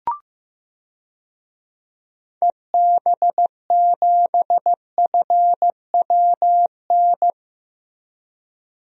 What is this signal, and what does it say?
Morse code, heard as a keyed tone, about 15 wpm, 700 Hz. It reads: EB7FWN
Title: DXCC-100 Call Signs - 6 Characters - Repeated 2x - 15wpm